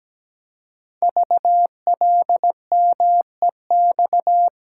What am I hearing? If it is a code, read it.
Morse code, heard as a keyed tone, about 17 wpm, 700 Hz. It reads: VLMEX
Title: Code Groups - 5 Letters - 17wpm